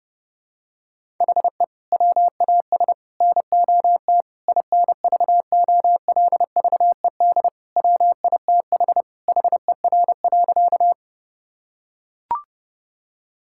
Morse code, heard as a keyed tone, about 30 wpm, 700 Hz. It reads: HE WAS NOT INVOLVED WITH HER.
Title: Sentences from Top 1000 Words - Review Part 5 - 30wpm